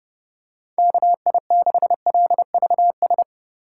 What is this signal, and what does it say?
Morse code, heard as a keyed tone, about 29 wpm, 700 Hz. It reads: KI6LVS
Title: USA Call Signs - 3x Character Spacing - 30wpm